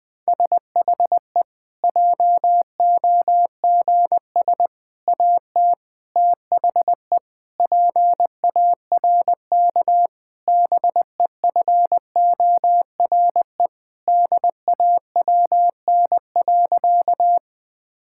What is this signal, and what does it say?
Morse code, heard as a keyed tone, about 20 wpm, 700 Hz. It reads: SHE JOGS AT THE PARK BEFORE DAWN.
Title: Difficult Sentences with Hints - 20wpm